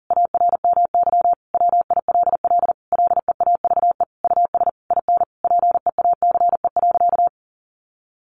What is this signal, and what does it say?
Morse code, heard as a keyed tone, about 40 wpm, 700 Hz. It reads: ARMY WILL LEAVE US IN PEACE.
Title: Sentences from Top 1500 Words 40wpm